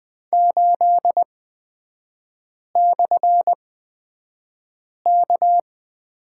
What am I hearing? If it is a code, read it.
Morse code, heard as a keyed tone, about 20 wpm, spaced as for 5 wpm, 690 Hz. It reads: 8/K
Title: 229 Lesson 37e - Letter x - Three Characters 20wpm